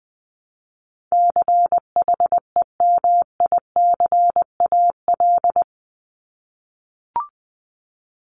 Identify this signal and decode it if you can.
Morse code, heard as a keyed tone, about 20 wpm, 700 Hz. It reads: CHEMICAL
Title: Top 1001-1250 Words 20wpm